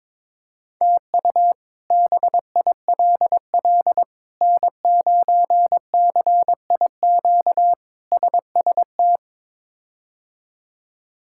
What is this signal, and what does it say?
Morse code, heard as a keyed tone, about 22 wpm, 700 Hz. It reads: TU BILL N9CIQ SST
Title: K1USN SST QSO Practice - 22wpm